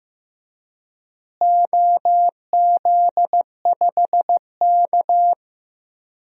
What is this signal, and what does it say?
Morse code, heard as a keyed tone, about 15 wpm, 700 Hz. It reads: OZ5K
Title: DXCC-100 Call Signs - 4 Characters - Repeated 3x - 15wpm